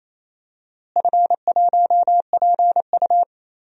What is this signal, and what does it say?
Morse code, heard as a keyed tone, about 28 wpm, 700 Hz. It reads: F1PU
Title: DXCC-100 Call Signs - 4 Characters - Repeated 3x - 28wpm